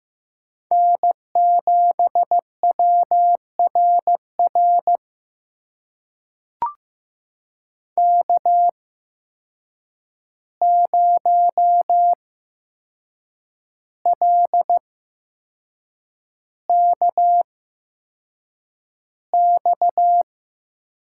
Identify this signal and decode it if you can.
Morse code, heard as a keyed tone, about 15 wpm, spaced as for 6 wpm, 700 Hz. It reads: N7WRR K0LKX
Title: USA Call Signs - 5 Characters - 8x Character Spacing - 15wpm